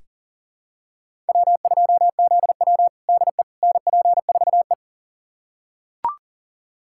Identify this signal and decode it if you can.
Morse code, heard as a keyed tone, about 40 wpm, 690 Hz. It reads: W2ZW DE NP4E
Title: USA Callsign DE Callsign - 4 Characters - 40wpm